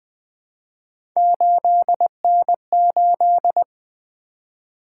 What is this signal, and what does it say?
Morse code, heard as a keyed tone, about 20 wpm, 700 Hz. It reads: 8N8
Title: Three Letter-Number Combinations - 20wpm